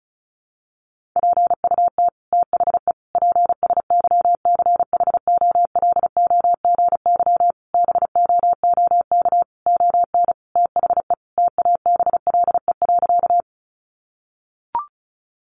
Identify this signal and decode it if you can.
Morse code, heard as a keyed tone, about 34 wpm, 700 Hz. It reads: PUT THE PSYCHOLOGY BOOK ON THE TABLE.
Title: Sentences from Top 1400 Words 35wpm